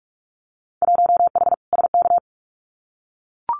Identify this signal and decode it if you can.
Morse code, heard as a keyed tone, about 44 wpm, 700 Hz. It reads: 1H SK
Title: Winter Field Day Exchanges 45wpm